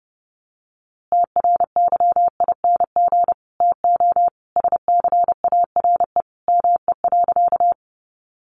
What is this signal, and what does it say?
Morse code, heard as a keyed tone, about 30 wpm, 700 Hz. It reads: TRYING TO SCARE ME.
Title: A Sound Of Thunder 30wpm